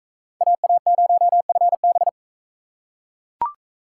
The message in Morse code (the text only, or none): AA0FB